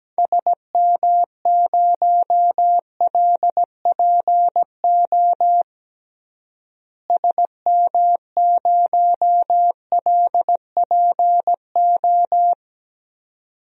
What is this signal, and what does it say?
Morse code, heard as a keyed tone, about 17 wpm, 700 Hz. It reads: SM0LPO SM0LPO
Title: DXCC-100 Call Signs - 6 Characters - Repeated 2x - 17wpm